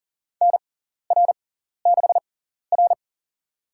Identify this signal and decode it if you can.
Morse code, heard as a keyed tone, about 38 wpm, spaced as for 14 wpm, 700 Hz. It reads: NR6R
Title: USA Call Signs - 4 Characters - 6x Character Spacing - 40wpm